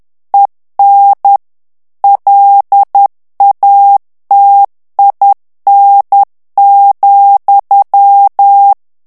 Morse code, heard as a keyed tone, about 10 wpm, 800 Hz. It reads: EN LATIN,
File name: CW / FICHIERS-MP3 / FRANCAIS TEXTE CLAIR 2 VITESSE 10 MOTS MINUTES